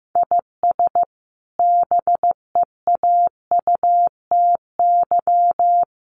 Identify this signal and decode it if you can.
Morse code, heard as a keyed tone, about 15 wpm, 700 Hz. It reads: IS BEAUTY
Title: Sentences from Top 1300 Words 15wpm